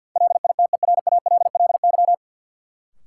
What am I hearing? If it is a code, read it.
Morse code, heard as a keyed tone, about 53 wpm, 700 Hz. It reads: LITERALLY
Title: Sentences from Top 1400 Words - New Word Drill - 50wpm